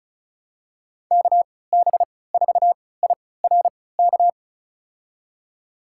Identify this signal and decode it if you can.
Morse code, heard as a keyed tone, about 36 wpm, spaced as for 20 wpm, 700 Hz. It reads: KB4IRK
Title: USA Call Signs - 6 Characters - 3x Character Spacing - 35wpm